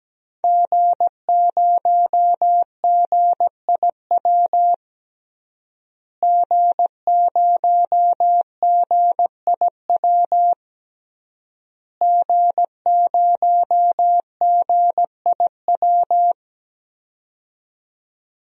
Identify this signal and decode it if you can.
Morse code, heard as a keyed tone, about 17 wpm, 700 Hz. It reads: G0GIW G0GIW G0GIW
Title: DXCC-100 Call Signs - 5 Characters - Repeated 3x - 17wpm